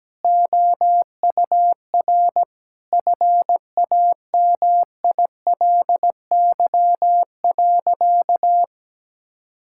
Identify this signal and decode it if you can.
Morse code, heard as a keyed tone, about 17 wpm, 700 Hz. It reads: OUR FAMILY.
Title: Sentences from Top 1000 Words - Review Part 6 - 17wpm